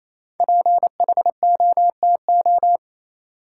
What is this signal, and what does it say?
Morse code, heard as a keyed tone, about 28 wpm, 700 Hz. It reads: PHOTO
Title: Five Letter Words - 28wpm